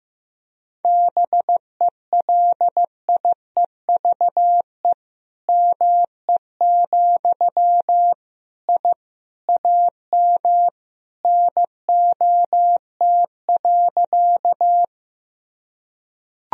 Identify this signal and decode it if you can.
Morse code, heard as a keyed tone, about 15 wpm, 700 Hz. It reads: BELIEVE ME, I AM NOT.
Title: Sentences from Top 1000 Words - Review Part 2 - 15wpm